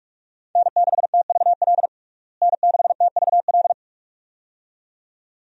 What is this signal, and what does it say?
Morse code, heard as a keyed tone, about 46 wpm, 700 Hz. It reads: N6TVL N6TVL
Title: USA Call Signs - 5 Characters - Repeated 2x - 45wpm